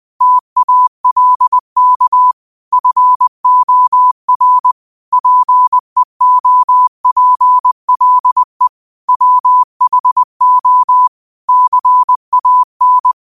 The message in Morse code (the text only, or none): TALK FOR PEOPLE WHO CAN